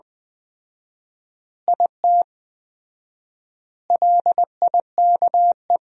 Morse code, heard as a keyed tone, about 20 wpm, 700 Hz. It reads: IT LIKE
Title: 4x Word-Spacing - Sentences from Top 300 Words - 20wpm